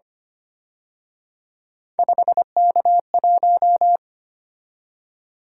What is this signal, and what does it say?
Morse code, heard as a keyed tone, about 29 wpm, 700 Hz. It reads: 5K1
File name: Speed-Racing - Three Letter-Number Combinations - 25wpm